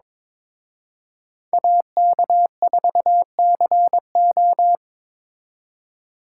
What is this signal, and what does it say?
Morse code, heard as a keyed tone, about 26 wpm, 700 Hz. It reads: AK4CO
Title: Speed Racing - USA 2x2 Call Signs 22wpm